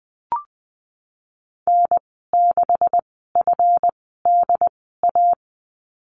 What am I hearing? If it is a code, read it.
Morse code, heard as a keyed tone, about 20 wpm, spaced as for 15 wpm, 700 Hz. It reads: N6FDA
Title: 167 Lesson 27g - Number 7 - Call Signs 20wpm